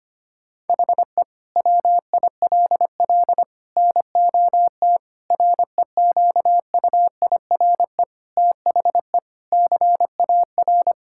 Morse code, heard as a keyed tone, about 25 wpm, 700 Hz. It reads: HE WILL NOT REQUIRE THE CAR
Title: Sentences from Top 1000 Words - Review Part 6 - 25wpm